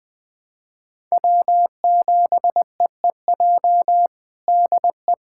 Morse code, heard as a keyed tone, about 21 wpm, 700 Hz. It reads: W7EEJ DE
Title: USA Callsign DE Callsign 20wpm